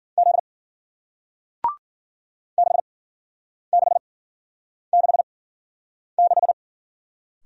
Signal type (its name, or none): Morse code